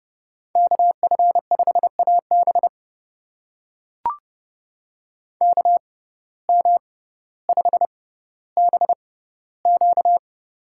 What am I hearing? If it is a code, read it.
Morse code, heard as a keyed tone, about 31 wpm, spaced as for 15 wpm, 700 Hz. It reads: KF5AB KM5BQ
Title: USA Call Signs - 5 Characters - 6x Character Spacing - 30wpm